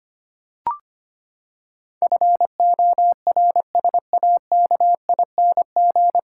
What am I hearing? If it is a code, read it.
Morse code, heard as a keyed tone, about 25 wpm, 700 Hz. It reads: FORSAKING
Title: Pangram Challenge v2 - 25wpm